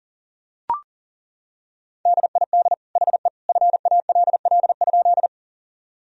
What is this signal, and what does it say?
Morse code, heard as a keyed tone, about 40 wpm, 700 Hz. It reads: DID HE FALL?
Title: Sentences from Top 1000 Words - Review Part 2 - 40wpm